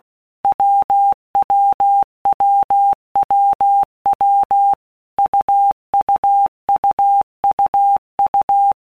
Morse code, heard as a keyed tone, about 16 wpm, 800 Hz. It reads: WWWWW UUUUU
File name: All Groups 1-minute Blocks / Groups@15 / AWUNDG